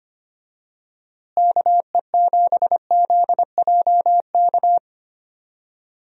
Morse code, heard as a keyed tone, about 25 wpm, 700 Hz. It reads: KE7ZJK
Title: USA Call Signs - 6 Characters - Repeated 2x - 25wpm